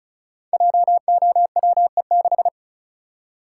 Morse code, heard as a keyed tone, about 34 wpm, 700 Hz. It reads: JOWE6
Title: Code Groups - 5 Letter:Numbers - 35wpm